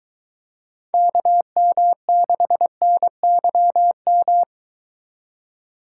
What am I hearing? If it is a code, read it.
Morse code, heard as a keyed tone, about 22 wpm, 700 Hz. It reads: KM6NYM